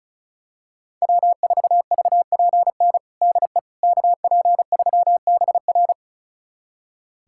Morse code, heard as a keyed tone, about 34 wpm, 700 Hz. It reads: W4VPN DE KP3BR
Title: USA Callsign DE Callsign - 5 Characters - 35wpm